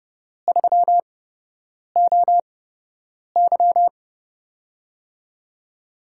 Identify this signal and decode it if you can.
Morse code, heard as a keyed tone, about 30 wpm, spaced as for 8 wpm, 700 Hz. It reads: 3OY
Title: USA Call Signs - 5 Characters - 8x Character Spacing - 30wpm